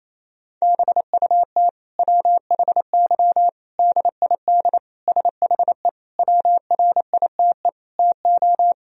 Morse code, heard as a keyed tone, about 28 wpm, 700 Hz. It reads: BUT WHY DID SHE WRITE TO
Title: Sentences from Top 1000 Words - Review Part 1 - 28wpm